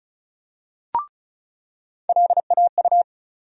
Morse code, heard as a keyed tone, about 36 wpm, 700 Hz. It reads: LAU
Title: USA Call Sign Postfixes - 3 Letters 35wpm